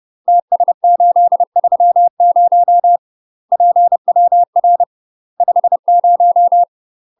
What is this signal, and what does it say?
Morse code, heard as a keyed tone, about 30 wpm, 700 Hz. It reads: TS830 PWR 50
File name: UK MASTER QSOs TRAINING / 30WPM / UK MASTER QSOs 0014